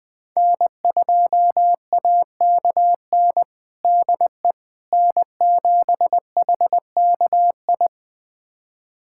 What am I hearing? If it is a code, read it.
Morse code, heard as a keyed tone, about 20 wpm, 700 Hz. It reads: N2AKN DE N7HKI
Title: USA Callsign DE Callsign - 5 Characters - 20wpm